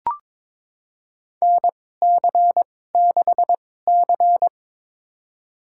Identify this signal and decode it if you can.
Morse code, heard as a keyed tone, about 22 wpm, spaced as for 16 wpm, 700 Hz. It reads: NC6C
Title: USA Call Signs - 4 Characters - 2x Character Spacing - 22wpm